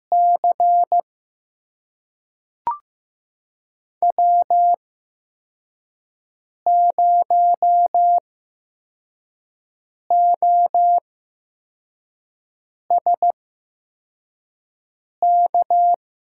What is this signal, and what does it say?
Morse code, heard as a keyed tone, about 15 wpm, spaced as for 4 wpm, 700 Hz. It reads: C W0OSK
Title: USA Call Signs - 8x Character Spacing - 15wpm